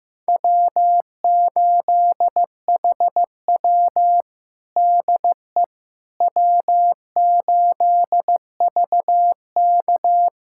Morse code, heard as a keyed tone, about 15 wpm, 700 Hz. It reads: W8HW DE W8VK
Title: USA Callsign DE Callsign - 4 Characters - 15wpm